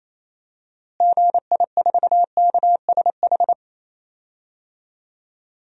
Morse code, heard as a keyed tone, about 28 wpm, 700 Hz. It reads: GI4KSH
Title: DXCC-100 Call Signs - 6 Characters - Repeated 2x - 28wpm